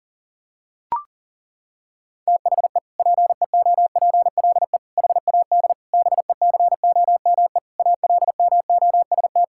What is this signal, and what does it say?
Morse code, heard as a keyed tone, about 40 wpm, 700 Hz. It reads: THE PEOPLE HAD BECOME ALMOST
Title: Sentences from Top 1000 Words - Review Part 6 - 40wpm